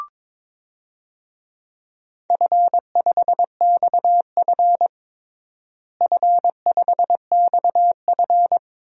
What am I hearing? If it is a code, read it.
Morse code, heard as a keyed tone, about 22 wpm, 700 Hz. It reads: F5XF F5XF
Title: DXCC-100 Call Signs - 4 Characters - Repeated 3x - 22wpm